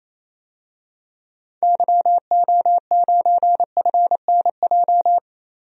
Morse code, heard as a keyed tone, about 27 wpm, 700 Hz. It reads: YO9FNJ
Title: DXCC-100 Call Signs - 6 Characters - 28wpm